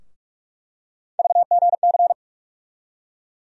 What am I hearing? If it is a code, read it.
Morse code, heard as a keyed tone, about 46 wpm, 700 Hz. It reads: VGC
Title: USA Call Sign Postfixes - 3 Letters 45wpm